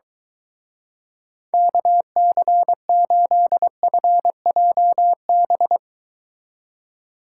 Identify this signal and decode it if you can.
Morse code, heard as a keyed tone, about 23 wpm, 700 Hz. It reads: KC8FJB